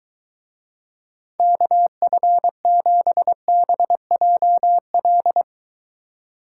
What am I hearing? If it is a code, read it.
Morse code, heard as a keyed tone, about 24 wpm, 700 Hz. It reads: KF7BJL